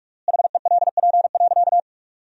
Morse code, heard as a keyed tone, about 47 wpm, 700 Hz. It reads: HELP.
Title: 3x Word-Spacing - Sentences from Top 200 Words - 45wpm